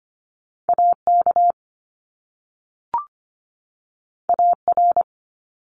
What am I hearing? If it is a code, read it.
Morse code, heard as a keyed tone, about 25 wpm, 700 Hz. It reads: AK AR
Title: US States 25wpm